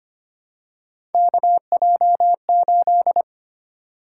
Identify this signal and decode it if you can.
Morse code, heard as a keyed tone, about 25 wpm, 700 Hz. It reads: KJ8
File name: Three Letter-Number Combinations - 25wpm